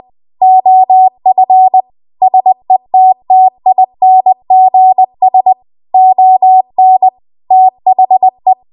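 Morse code, heard as a keyed tone, about 20 wpm, 750 Hz. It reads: OF SETTINGS ON THE
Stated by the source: Morse code practice at 20 WPM.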